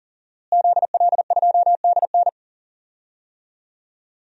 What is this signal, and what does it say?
Morse code, heard as a keyed tone, about 39 wpm, 700 Hz. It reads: ZL2DN
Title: DXCC-100 Call Signs - 5 Characters - Repeated 2x - 40wpm